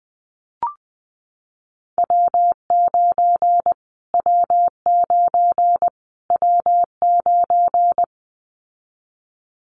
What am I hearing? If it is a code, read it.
Morse code, heard as a keyed tone, about 20 wpm, 700 Hz. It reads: W9 W9 W9
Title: Callsign Prefixes for Top 25 DXCC - Repeated 3x - 20wpm